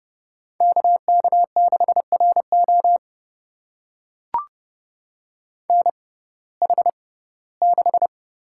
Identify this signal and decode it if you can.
Morse code, heard as a keyed tone, about 30 wpm, spaced as for 18 wpm, 700 Hz. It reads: KK6RO NH6
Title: USA Call Signs - 5 Characters - 6x Character Spacing - 30wpm